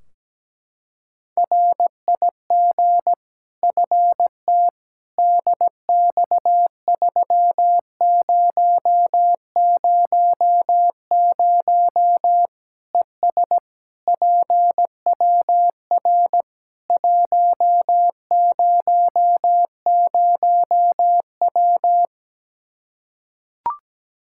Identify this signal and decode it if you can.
Morse code, heard as a keyed tone, about 17 wpm, 700 Hz. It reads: RIG FT DX3000 ES PWR 100W
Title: Practice QSO - Exchange 2 - Info - 17wpm